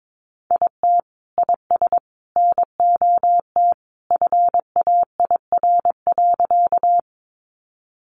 Morse code, heard as a keyed tone, about 22 wpm, 700 Hz. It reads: IT IS NOT FAIR.
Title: Sentences from Top 1300 Words 22wpm